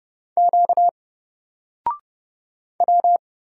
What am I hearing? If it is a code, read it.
Morse code, heard as a keyed tone, about 30 wpm, 700 Hz. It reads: Q W
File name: Single-Letter-Number-30wpm